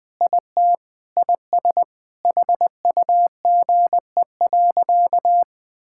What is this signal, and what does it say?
Morse code, heard as a keyed tone, about 20 wpm, 700 Hz. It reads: IT IS HUGE.
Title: Sentences with 5-Word Phrases - 20wpm